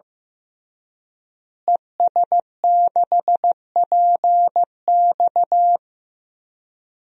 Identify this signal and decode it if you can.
Morse code, heard as a keyed tone, about 15 wpm, 700 Hz. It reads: ES6PX
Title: DXCC-100 Call Signs - 5 Characters - Repeated 3x - 15wpm